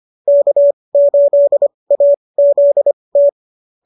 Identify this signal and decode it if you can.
Morse code, heard as a keyed tone, about 25 wpm, spaced as for 20 wpm, 590 Hz. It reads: K8AZT